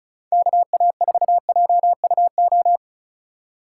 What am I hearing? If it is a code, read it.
Morse code, heard as a keyed tone, about 42 wpm, 700 Hz. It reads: KA4JUO
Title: Speed Racing - USA 2x3 Call Signs 35wpm